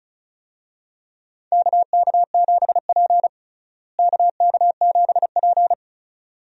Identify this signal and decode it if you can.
Morse code, heard as a keyed tone, about 36 wpm, 700 Hz. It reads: KK7P KK7P
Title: USA Call Signs - 4 Characters - Repeated 2x - 35wpm